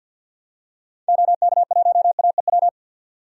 Morse code, heard as a keyed tone, about 50 wpm, 700 Hz. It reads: OK1AEW
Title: DXCC-100 Call Signs - 6 Characters - 50wpm